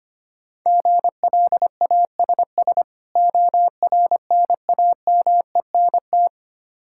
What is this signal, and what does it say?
Morse code, heard as a keyed tone, about 25 wpm, 700 Hz. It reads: GLASS ORNAMENT
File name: practice / 25wpm / Difficult Words with Hints - 25wpm